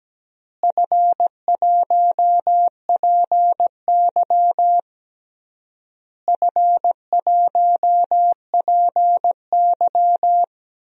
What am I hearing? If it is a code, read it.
Morse code, heard as a keyed tone, about 17 wpm, 700 Hz. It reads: F1PY F1PY
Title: DXCC-100 Call Signs - 4 Characters - Repeated 3x - 17wpm